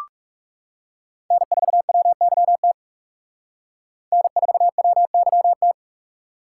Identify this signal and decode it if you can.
Morse code, heard as a keyed tone, about 37 wpm, 700 Hz. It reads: N4WYT N4WYT
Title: Speed Racing - USA 1x3 Call Signs 30wpm